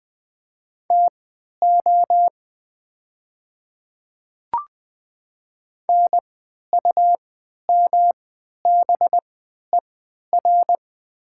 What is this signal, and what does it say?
Morse code, heard as a keyed tone, about 20 wpm, spaced as for 11 wpm, 700 Hz. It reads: TO NUMBER
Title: 3x Character Spacing - Top 100 Words - 20wpm